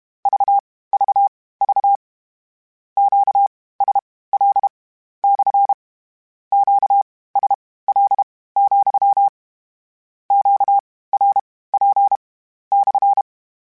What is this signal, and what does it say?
Morse code, heard as a keyed tone, about 32 wpm, spaced as for 18 wpm, 800 Hz. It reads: VVV QSL / QSL, QRP /
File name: Q Calls / Q Calls-mp3s / Q CALLS SPSL @30